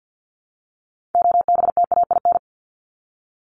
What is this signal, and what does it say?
Morse code, heard as a keyed tone, about 50 wpm, 700 Hz. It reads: OBTAIN
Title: ED Suffix - Top 100 Words with Base Words 50wpm